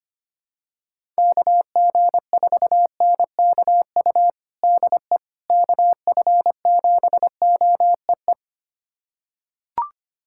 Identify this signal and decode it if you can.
Morse code, heard as a keyed tone, about 25 wpm, 700 Hz. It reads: KG4NKU DE KF7OEE
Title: USA Callsign DE Callsign - 6 Characters - 25wpm